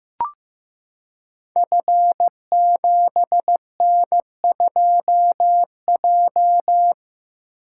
Morse code, heard as a keyed tone, about 15 wpm, 700 Hz. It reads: F7N2J
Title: Code Groups - 5 Letter:Numbers - 15wpm